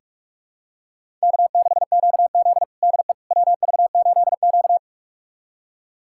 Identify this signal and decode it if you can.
Morse code, heard as a keyed tone, about 42 wpm, 700 Hz. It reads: K6QZ DE WV8Q
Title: USA Callsign DE Callsign - 4 Characters - 45wpm